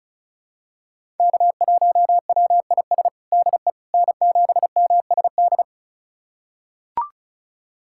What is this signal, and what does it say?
Morse code, heard as a keyed tone, about 36 wpm, 700 Hz. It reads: K1WIS DE N7MSD
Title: USA Callsign DE Callsign 35wpm